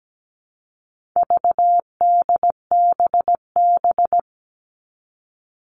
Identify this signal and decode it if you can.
Morse code, heard as a keyed tone, about 17 wpm, 700 Hz. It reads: VDBB
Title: All Cut Numbers - 4-Digit 17wpm